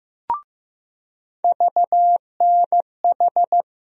Morse code, heard as a keyed tone, about 15 wpm, 700 Hz. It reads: VNH
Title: USA Call Sign Postfixes - 3 Letters 15wpm